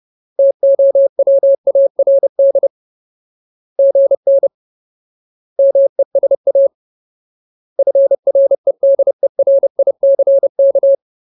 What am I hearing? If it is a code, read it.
Morse code, heard as a keyed tone, about 30 wpm, 550 Hz. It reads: TOWARD GN MESA FREDERICK
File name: Ultimate Word List / 30WPM Ultimate